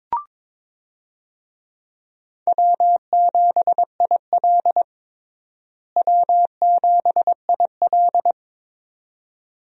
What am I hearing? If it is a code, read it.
Morse code, heard as a keyed tone, about 22 wpm, 700 Hz. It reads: W7IL W7IL
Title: USA Call Signs - 4 Characters - Repeated 2x - 22wpm